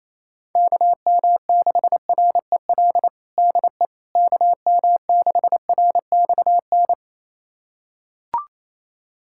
Morse code, heard as a keyed tone, about 28 wpm, 700 Hz. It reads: KM6REL DE KM6RXN
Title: USA Callsign DE Callsign - 6 Characters - 28wpm